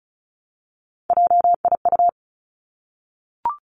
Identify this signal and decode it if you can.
Morse code, heard as a keyed tone, about 36 wpm, 700 Hz. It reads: JIU
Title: Three Challenging Letter-Number Combinations 35wpm